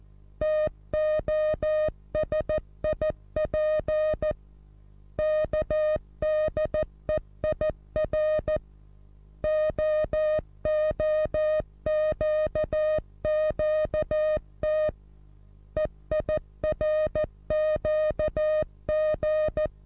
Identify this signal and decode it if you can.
Morse code, heard as a keyed tone, about 14 wpm, 620 Hz. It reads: TOSIP KDEIR OOQQT EIRQG
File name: repos / Morsekurs / DJ4UF Morsen27